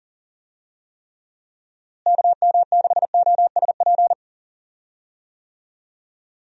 Morse code, heard as a keyed tone, about 39 wpm, 700 Hz. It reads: KM6OSP
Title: USA Call Signs - 6 Characters - Repeated 2x - 40wpm